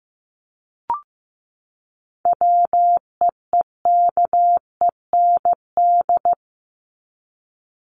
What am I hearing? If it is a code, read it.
Morse code, heard as a keyed tone, about 15 wpm, 700 Hz. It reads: WEEKEND
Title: Top 1751-2000 Words 15wpm